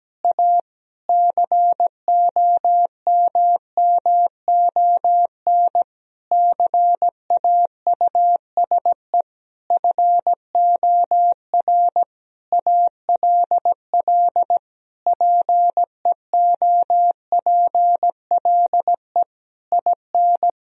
Morse code, heard as a keyed tone, about 17 wpm, 700 Hz. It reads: A COMMON CAUSE FOR ALL PEOPLE IN
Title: Sentences from Top 1000 Words - Review Part 3 - 17wpm